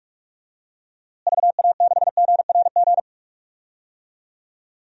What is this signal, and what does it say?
Morse code, heard as a keyed tone, about 42 wpm, 700 Hz. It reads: VA6GRG
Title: DXCC-100 Call Signs - 6 Characters - Repeated 3x - 45wpm